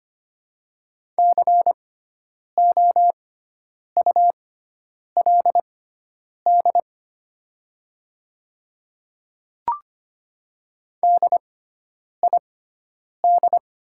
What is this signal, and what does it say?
Morse code, heard as a keyed tone, about 25 wpm, spaced as for 9 wpm, 690 Hz. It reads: COULD DID
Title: 6x Character Spacing - Top 300 Words - 25wpm